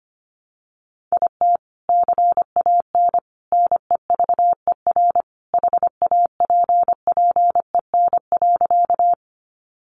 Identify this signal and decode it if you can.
Morse code, heard as a keyed tone, about 25 wpm, 700 Hz. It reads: IT CAN NEVER HAPPEN.
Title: Sentences from Top 400 Words 25wpm